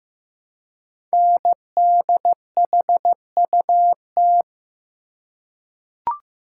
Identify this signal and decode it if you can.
Morse code, heard as a keyed tone, about 15 wpm, 700 Hz. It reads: NDHUT
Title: Code Groups - 5 Letters - 15wpm